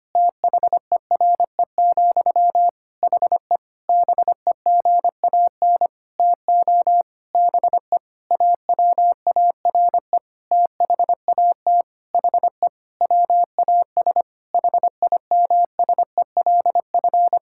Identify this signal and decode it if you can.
Morse code, heard as a keyed tone, about 25 wpm, 700 Hz. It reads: THERE, HE BEGAN TO BE AWARE THAT HE WAS HIMSELF
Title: Sentences from Top 1600 Words 25wpm